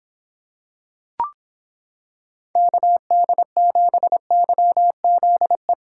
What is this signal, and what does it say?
Morse code, heard as a keyed tone, about 26 wpm, 700 Hz. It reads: KD7YZE